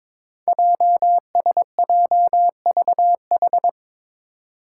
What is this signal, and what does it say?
Morse code, heard as a keyed tone, about 22 wpm, 700 Hz. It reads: JSJVH
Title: Five Challenging Letter-Number Combinations 22wpm